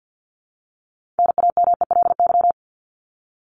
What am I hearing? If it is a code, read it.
Morse code, heard as a keyed tone, about 50 wpm, 700 Hz. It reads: NAMELY
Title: LY Suffix - Top 100 Words with Base Words 50wpm